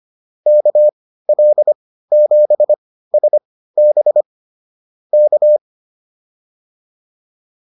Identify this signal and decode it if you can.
Morse code, heard as a keyed tone, about 25 wpm, spaced as for 15 wpm, 600 Hz. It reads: KL7SB K
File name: QSO107 15